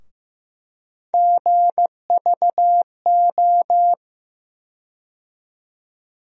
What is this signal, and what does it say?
Morse code, heard as a keyed tone, about 15 wpm, 700 Hz. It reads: GVO
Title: USA Call Sign Postfixes - 3 Letters 15wpm